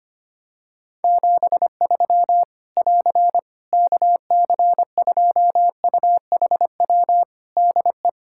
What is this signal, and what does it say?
Morse code, heard as a keyed tone, about 25 wpm, 700 Hz. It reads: 73 <AR> KC2UHW DE
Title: Practice QSO - Exchange 4 - Ending - 25wpm